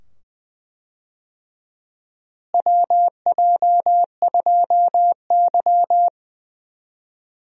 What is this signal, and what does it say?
Morse code, heard as a keyed tone, about 21 wpm, 700 Hz. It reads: WJ2Y
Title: USA Call Signs - 4 Characters - Repeated 2x - 20wpm